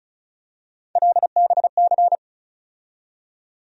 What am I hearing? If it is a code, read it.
Morse code, heard as a keyed tone, about 36 wpm, 700 Hz. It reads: LBC
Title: USA Call Sign Postfixes - 3 Letters 35wpm